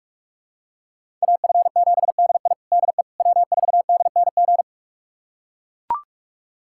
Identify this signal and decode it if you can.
Morse code, heard as a keyed tone, about 41 wpm, 700 Hz. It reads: AF7DI DE W4DNG
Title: USA Callsign DE Callsign - 5 Characters - 45wpm